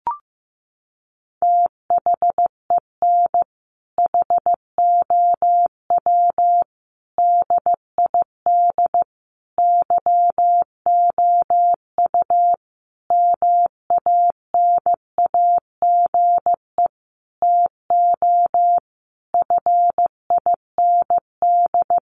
Morse code, heard as a keyed tone, about 15 wpm, 700 Hz. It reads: THEN HOW DID YOU MANAGE TO FIND
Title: Sentences from Top 1400 Words 15wpm